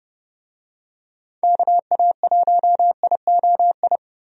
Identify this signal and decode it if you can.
Morse code, heard as a keyed tone, about 30 wpm, 700 Hz. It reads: KA1IOI
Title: USA Call Signs - 6 Characters - 30wpm